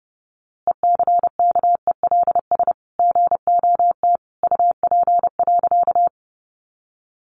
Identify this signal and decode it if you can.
Morse code, heard as a keyed tone, about 30 wpm, 700 Hz. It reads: ECKELS GOT UP.
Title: A Sound Of Thunder 30wpm